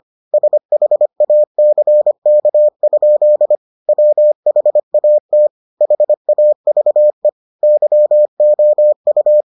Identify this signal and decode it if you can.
Morse code, heard as a keyed tone, about 25 wpm, 600 Hz. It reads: SHACK? WHAT HAVE YOU